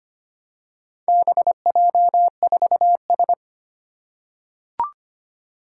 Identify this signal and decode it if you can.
Morse code, heard as a keyed tone, about 25 wpm, 700 Hz. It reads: BJ4S
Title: Four Challenging Letter-Number Combinations 25wpm